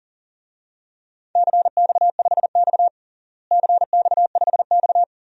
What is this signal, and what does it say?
Morse code, heard as a keyed tone, about 40 wpm, 700 Hz. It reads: CX5X CX5X
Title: DXCC-100 Call Signs - 4 Characters - Repeated 2x - 40wpm